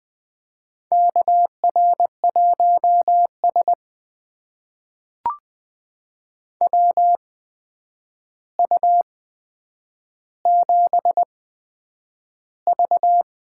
Morse code, heard as a keyed tone, about 20 wpm, spaced as for 9 wpm, 700 Hz. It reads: KR1S WU7V
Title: USA Call Signs - 4 Characters - 8x Character Spacing - 20wpm